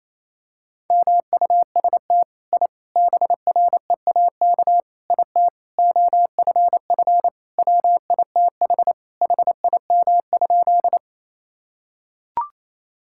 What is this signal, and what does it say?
Morse code, heard as a keyed tone, about 28 wpm, 700 Hz. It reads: MUST I BREAK IT OFF WITH HIM?
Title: Sentences from Top 1000 Words - Review Part 6 - 28wpm